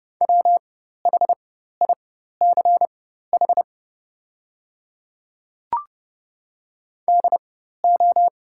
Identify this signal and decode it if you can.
Morse code, heard as a keyed tone, about 30 wpm, spaced as for 14 wpm, 700 Hz. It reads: WHICH DO
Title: 4x Character Spacing - Top 100 Words - 30wpm